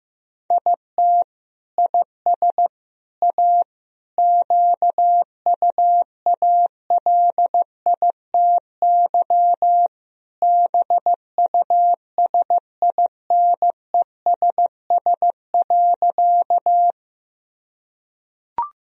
Morse code, heard as a keyed tone, about 15 wpm, 700 Hz. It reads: IT IS A QUALITY BUSINESS.
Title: Sentences from Top 1000 Words - Review Part 4 - 15wpm